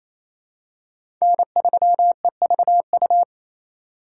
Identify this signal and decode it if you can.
Morse code, heard as a keyed tone, about 28 wpm, 700 Hz. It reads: N3EVU
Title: USA Call Signs - 5 Characters - 4x Character Spacing - 28wpm